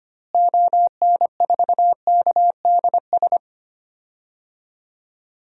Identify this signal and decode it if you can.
Morse code, heard as a keyed tone, about 25 wpm, 700 Hz. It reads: ON4KDS
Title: DXCC-100 Call Signs - 6 Characters - Repeated 3x - 25wpm